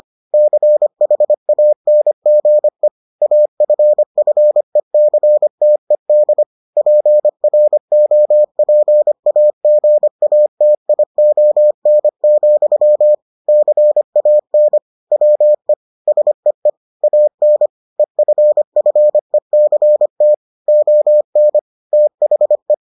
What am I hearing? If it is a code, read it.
Morse code, heard as a keyed tone, about 25 wpm, 600 Hz. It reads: CHANGE AFFECTED PROPAGATION, CAN WE SEE AN EFFECT ON THE